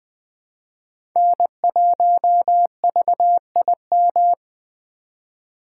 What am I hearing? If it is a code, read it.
Morse code, heard as a keyed tone, about 20 wpm, 700 Hz. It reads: N1VIM
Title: USA Call Signs - 5 Characters - 4x Character Spacing - 20wpm